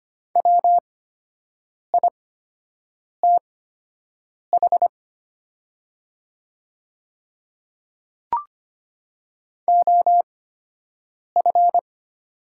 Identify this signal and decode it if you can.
Morse code, heard as a keyed tone, about 25 wpm, spaced as for 7 wpm, 700 Hz. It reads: WITH OF
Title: 8x Character Spacing - Top 200 Words - 25wpm